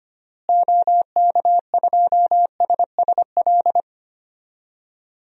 Morse code, heard as a keyed tone, about 25 wpm, 700 Hz. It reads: OK2SSL
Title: DXCC-100 Call Signs - 6 Characters - Repeated 2x - 25wpm